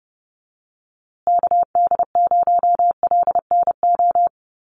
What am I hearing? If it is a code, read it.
Morse code, heard as a keyed tone, about 30 wpm, 700 Hz. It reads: KD0LNO
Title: USA Call Signs Repeated 3x 30wpm